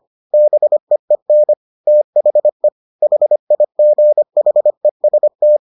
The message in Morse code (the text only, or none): BEEN THE HIGHEST